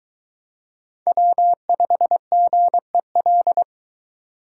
Morse code, heard as a keyed tone, about 22 wpm, 700 Hz. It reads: W5GEL